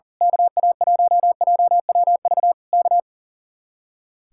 Morse code, heard as a keyed tone, about 38 wpm, 700 Hz. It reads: KA1JWV K
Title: Speed-Racing Common QSO Phrases Encore - 40wpm